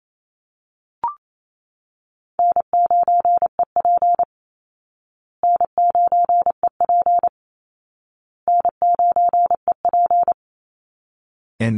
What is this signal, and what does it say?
Morse code, heard as a keyed tone, about 28 wpm, 700 Hz. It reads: N9EP N9EP N9EP
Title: Easy Call Signs Repeated 3x - 28wpm